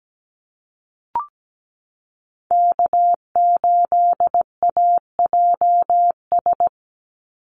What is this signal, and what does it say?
Morse code, heard as a keyed tone, about 17 wpm, 700 Hz. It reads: K8AJS
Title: Call Signs - 17wpm